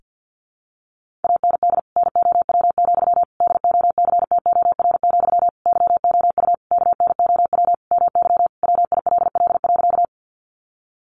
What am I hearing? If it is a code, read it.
Morse code, heard as a keyed tone, about 50 wpm, 700 Hz. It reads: AND NOW, DOCTOR, YOU KNOW MY WILL.
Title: Sentences from Top 1500 Words 50wpm